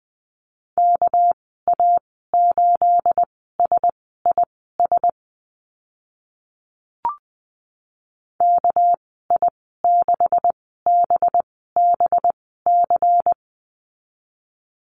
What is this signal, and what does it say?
Morse code, heard as a keyed tone, about 20 wpm, spaced as for 14 wpm, 700 Hz. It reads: KA8SIS KI6BBC
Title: 213 Lesson 34g - Number 8 - Call Signs 20wpm